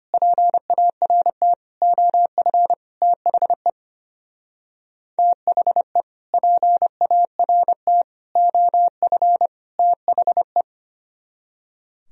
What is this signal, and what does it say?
Morse code, heard as a keyed tone, about 31 wpm, 700 Hz. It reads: PART OF THE THE PART OF THE
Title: Speed-Racing - Sets of 4 Words 25wpm